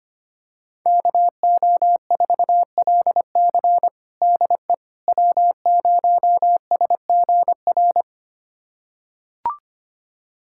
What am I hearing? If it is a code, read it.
Morse code, heard as a keyed tone, about 25 wpm, 700 Hz. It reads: KO4LC DE W0SGR
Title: USA Callsign DE Callsign - 5 Characters - 25wpm